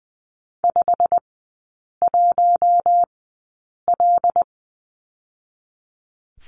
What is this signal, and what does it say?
Morse code, heard as a keyed tone, about 20 wpm, spaced as for 8 wpm, 690 Hz. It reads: 51L
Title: 087 Lesson 15e - Number 5 - Three Characters 20wpm